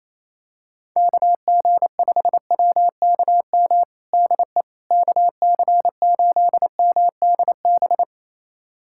Morse code, heard as a keyed tone, about 28 wpm, 700 Hz. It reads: KG5WKM DE KC8MDB
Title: USA Callsign DE Callsign 28wpm